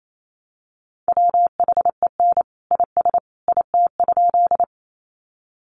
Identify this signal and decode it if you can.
Morse code, heard as a keyed tone, about 28 wpm, 700 Hz. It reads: WHEN IS IT?
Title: Sentences from Top 100 Words 28wpm